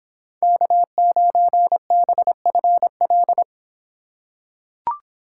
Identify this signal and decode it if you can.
Morse code, heard as a keyed tone, about 25 wpm, 700 Hz. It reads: K9BFL